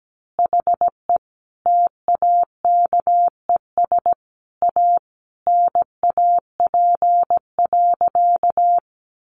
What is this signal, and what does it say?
Morse code, heard as a keyed tone, about 17 wpm, 700 Hz. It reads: HE TAKES A NAP.
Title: Easy Phrases 17wpm